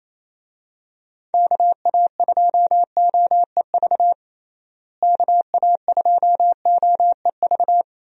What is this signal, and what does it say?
Morse code, heard as a keyed tone, about 27 wpm, 700 Hz. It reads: KA2OEV KA2OEV
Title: USA Call Signs - 6 Characters - Repeated 2x - 28wpm